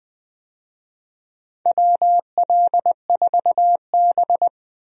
Morse code, heard as a keyed tone, about 20 wpm, 700 Hz. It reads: WL4B
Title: USA Call Signs - 4 Characters - 20wpm